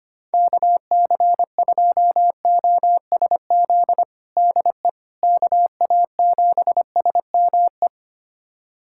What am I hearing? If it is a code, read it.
Morse code, heard as a keyed tone, about 25 wpm, 700 Hz. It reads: KC2OSZ DE KA7SME
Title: USA Callsign DE Callsign - 6 Characters - 25wpm